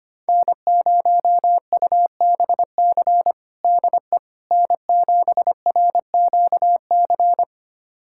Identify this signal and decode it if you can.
Morse code, heard as a keyed tone, about 25 wpm, 700 Hz. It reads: N0UBC DE N7RQC
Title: USA Callsign DE Callsign - 5 Characters - 25wpm